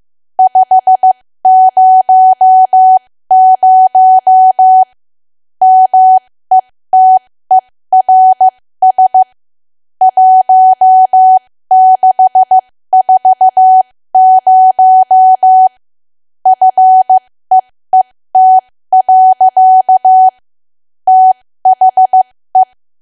Morse code, heard as a keyed tone, about 15 wpm, spaced as for 13 wpm, 750 Hz. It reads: 500 METERS 1640 FEET. THE
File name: cwa / W1AW-Level2 / 13WPM